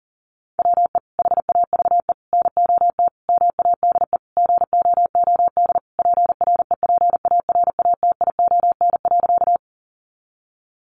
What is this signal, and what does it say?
Morse code, heard as a keyed tone, about 40 wpm, 700 Hz. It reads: WE HAVE NOT MADE GOOD PREPARATION.
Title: Sentences from Top 1400 Words 40wpm